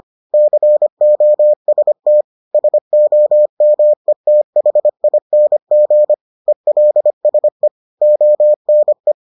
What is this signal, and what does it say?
Morse code, heard as a keyed tone, about 25 wpm, 600 Hz. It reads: COST SOMETHING ELSE ONE